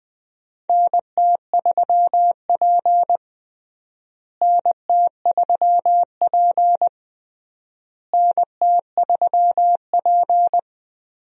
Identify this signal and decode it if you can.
Morse code, heard as a keyed tone, about 20 wpm, 700 Hz. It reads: NT3P NT3P NT3P
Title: USA Call Signs - 4 Characters - Repeated 3x - 20wpm